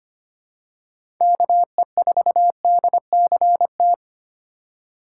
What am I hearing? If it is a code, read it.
Morse code, heard as a keyed tone, about 25 wpm, 700 Hz. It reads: KE4DCT
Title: USA Call Signs - 6 Characters - 2x Character Spacing - 25wpm